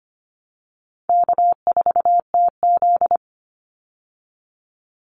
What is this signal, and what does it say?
Morse code, heard as a keyed tone, about 25 wpm, 700 Hz. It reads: K4TZ
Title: Call Signs Repeated 3x - 25wpm